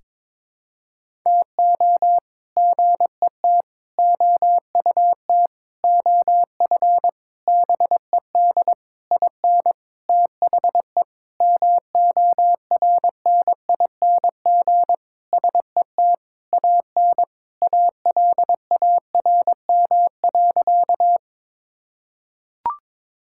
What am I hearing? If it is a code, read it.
Morse code, heard as a keyed tone, about 22 wpm, 700 Hz. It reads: TO GET OUT OF BED IN THE MORNING SET AN ALARM.
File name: practice / 22wpm / Sentences with 8-Word Phrases - 22wpm